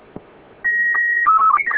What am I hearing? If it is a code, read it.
SSTV (slow-scan television), Robot 36.